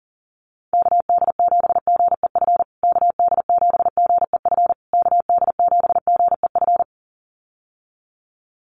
Morse code, heard as a keyed tone, about 40 wpm, 700 Hz. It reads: KD7GEF KD7GEF KD7GEF
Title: USA Call Signs Repeated 3x 40wpm